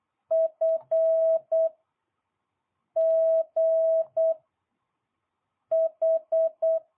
Morse code, heard as a keyed tone, about 8 wpm, 650 Hz. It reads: F G H